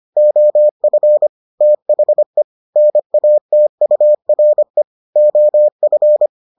Morse code, heard as a keyed tone, about 25 wpm, 600 Hz. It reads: OF THE NATURE OF